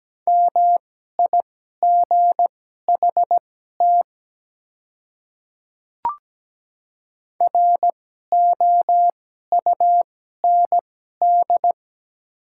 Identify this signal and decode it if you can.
Morse code, heard as a keyed tone, about 17 wpm, spaced as for 12 wpm, 700 Hz. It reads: MIGHT ROUND
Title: 2x Character Spacing - Top 300 Words - 17wpm